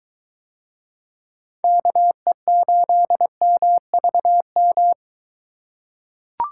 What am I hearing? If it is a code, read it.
Morse code, heard as a keyed tone, about 22 wpm, 700 Hz. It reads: KE8MVM